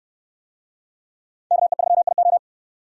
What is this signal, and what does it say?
Morse code, heard as a keyed tone, about 68 wpm, 700 Hz. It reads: KE3IO